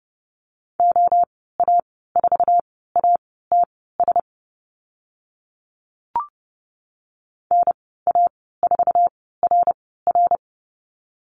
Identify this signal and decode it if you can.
Morse code, heard as a keyed tone, about 30 wpm, spaced as for 17 wpm, 700 Hz. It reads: OA4ATS NA4RR
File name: Lesson 15g - Number 5 - Call Signs 30wpm